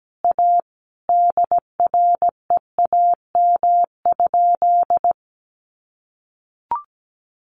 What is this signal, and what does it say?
Morse code, heard as a keyed tone, about 17 wpm, 700 Hz. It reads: A DREAM?
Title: Sentences from Top 1500 Words 17wpm